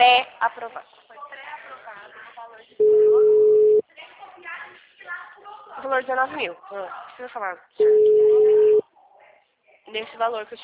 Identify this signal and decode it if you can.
Ringback tone (Europe, 425 Hz), with speech.